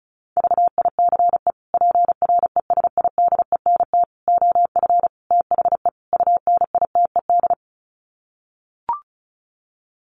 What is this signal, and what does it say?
Morse code, heard as a keyed tone, about 34 wpm, 700 Hz. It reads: VICE PRESIDENT OF THE UNITED
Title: Sets of 5 Words Encore2 35wpm